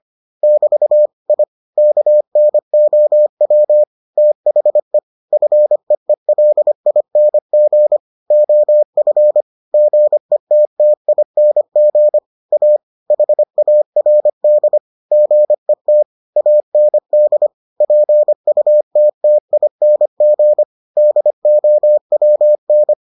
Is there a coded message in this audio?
Yes: Morse code from 0.4 to 22.9 s, heard as a keyed tone, about 25 wpm, 600 Hz. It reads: <BT> I KNOW THE FEELING OF GETTING A HARD GET AND PUTTING DOWN